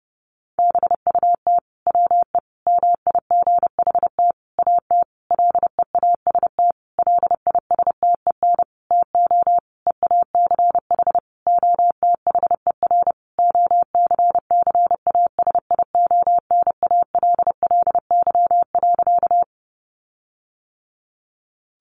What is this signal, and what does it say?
Morse code, heard as a keyed tone, about 30 wpm, 700 Hz. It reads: BUT WE MIGHT AT LEAST LISTEN TO EACH OTHER OCCASIONALLY.
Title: Sentences from Top 1300 Words 30wpm